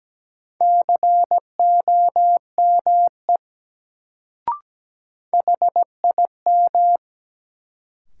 Morse code, heard as a keyed tone, about 17 wpm, 700 Hz. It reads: COME HIM
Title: Sentences from Top 100 Words - New Word Drill - 17wpm